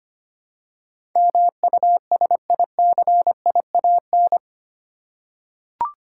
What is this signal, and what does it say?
Morse code, heard as a keyed tone, about 25 wpm, 700 Hz. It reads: MUSICIAN
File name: 8 Letter Words - First 3 Letters Given - 25wpm